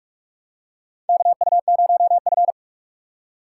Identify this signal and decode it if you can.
Morse code, heard as a keyed tone, about 46 wpm, 700 Hz. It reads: KU0F
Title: USA Call Signs - 4 Characters - 4x Character Spacing - 45wpm